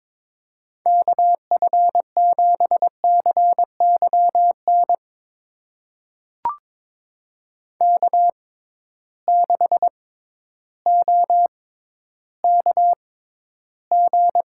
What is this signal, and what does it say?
Morse code, heard as a keyed tone, about 22 wpm, spaced as for 12 wpm, 700 Hz. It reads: KF7CYN K6OKG
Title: USA Call Signs - 6x Character Spacing - 22wpm